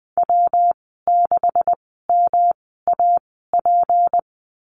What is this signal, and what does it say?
Morse code, heard as a keyed tone, about 20 wpm, spaced as for 15 wpm, 700 Hz. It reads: W6MAP
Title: 154 Lesson 25g - Letter p - Call Signs 20wpm